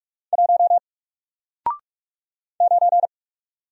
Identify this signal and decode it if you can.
Morse code, heard as a keyed tone, about 44 wpm, 700 Hz. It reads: 1 9
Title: Single Character (Letter:Number:Punctuation:Prosign) Alt 45wpm